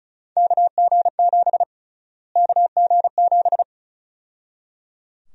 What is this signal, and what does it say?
Morse code, heard as a keyed tone, about 34 wpm, 700 Hz. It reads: KG7 KG7
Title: USA Call Sign Prefixes - Repeated 2x - 35wpm